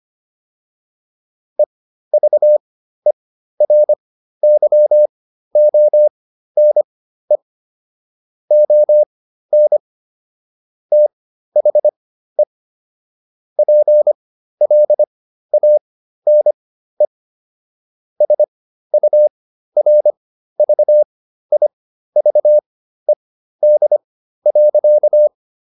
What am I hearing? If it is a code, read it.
Morse code, heard as a keyed tone, about 25 wpm, spaced as for 13 wpm, 600 Hz. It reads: EVERYONE ON THE PLANE SURVIVED.